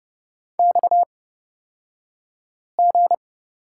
Morse code, heard as a keyed tone, about 30 wpm, spaced as for 5 wpm, 700 Hz. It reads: XG
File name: Lesson 37d - Letter x - Two Characters 30wpm